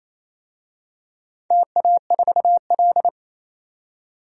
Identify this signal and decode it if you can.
Morse code, heard as a keyed tone, about 27 wpm, 700 Hz. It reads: TA4L
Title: DXCC-100 Call Signs - 4 Characters - 28wpm